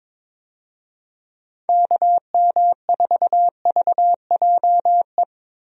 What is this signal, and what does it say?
Morse code, heard as a keyed tone, about 22 wpm, 700 Hz. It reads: KM4VJE
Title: USA Call Signs - 6 Characters - 22wpm